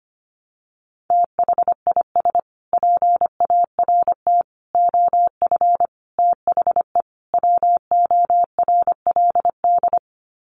Morse code, heard as a keyed tone, about 25 wpm, 700 Hz. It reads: THIS PART OF THE WORLD
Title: Sets of 5 Words Encore2 25wpm